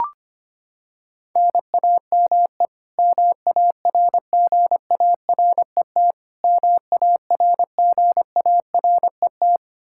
Morse code, heard as a keyed tone, about 25 wpm, 700 Hz. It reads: NAME MARGARET MARGARET
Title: K1USN SST QSO Practice 25wpm